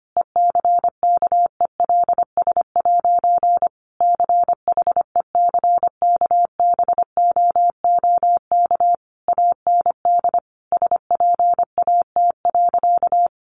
Morse code, heard as a keyed tone, about 25 wpm, 700 Hz. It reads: ECKELS' CHECKBOOK AND SPAT.
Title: A Sound Of Thunder 25wpm